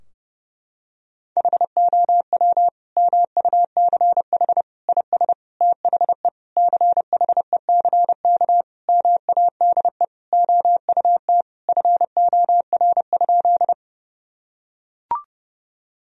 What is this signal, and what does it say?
Morse code, heard as a keyed tone, about 32 wpm, 700 Hz. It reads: HOW MUCH IS THE CHECK MADE OUT FOR?
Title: Sentences from Top 1000 Words - Review Part 3 - 30wpm